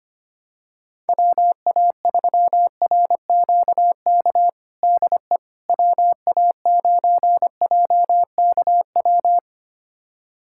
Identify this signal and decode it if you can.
Morse code, heard as a keyed tone, about 25 wpm, 700 Hz. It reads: WA3RQK DE WA9JKW
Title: USA Callsign DE Callsign - 6 Characters - 25wpm